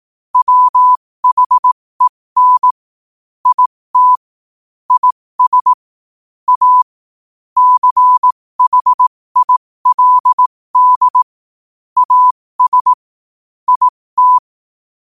Morse code, heard as a keyed tone, about 18 wpm, spaced as for 15 wpm, 1000 Hz. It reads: WHEN IT IS A CHILD AS IT
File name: qotdfiles / 15wpm